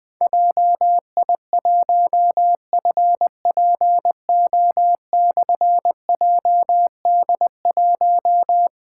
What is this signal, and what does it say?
Morse code, heard as a keyed tone, about 20 wpm, 700 Hz. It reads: JI1FPO/JD1
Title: Difficult Call Signs 20wpm